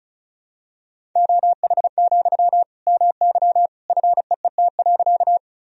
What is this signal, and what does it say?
Morse code, heard as a keyed tone, about 34 wpm, 700 Hz. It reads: OH, MY FEET.
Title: Sentences from Top 1000 Words - Review Part 1 - 35wpm